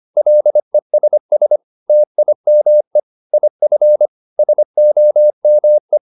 Morse code, heard as a keyed tone, about 25 wpm, 600 Hz. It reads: LESS TIME IF SOME
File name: SS319 25